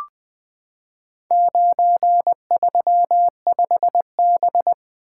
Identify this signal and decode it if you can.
Morse code, heard as a keyed tone, about 20 wpm, 700 Hz. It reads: 935B
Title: Four Letter-Number Combinations - 20wpm